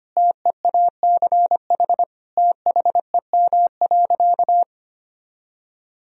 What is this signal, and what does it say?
Morse code, heard as a keyed tone, about 25 wpm, 700 Hz. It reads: TEACH THEM.
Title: Sentences from Top 1000 Words - Review Part 2 - 25wpm